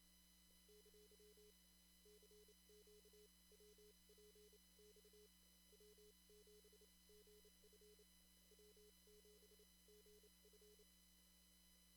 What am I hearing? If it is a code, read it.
Morse code, heard as a keyed tone, about 28 wpm, spaced as for 23 wpm, 420 Hz, with noise. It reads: ! CQWPX W7GF W7GF
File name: CW / kt5j wpx cw-op-k5pi